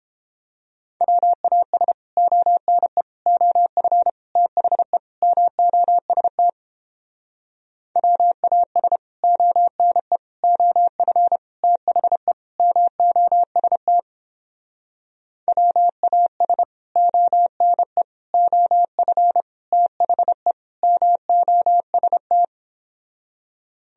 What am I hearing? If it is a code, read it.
Morse code, heard as a keyed tone, about 29 wpm, 700 Hz. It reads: WAS ONE OF THE MOST WAS ONE OF THE MOST WAS ONE OF THE MOST